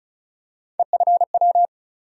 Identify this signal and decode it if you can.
Morse code, heard as a keyed tone, about 36 wpm, 700 Hz. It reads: EFW
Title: USA Call Sign Postfixes - 3 Letters 35wpm